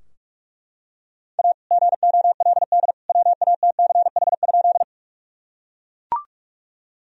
Morse code, heard as a keyed tone, about 42 wpm, 700 Hz. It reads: A GOLD WATCH?
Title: Sentences from Top 1000 Words - Review Part 3 - 45wpm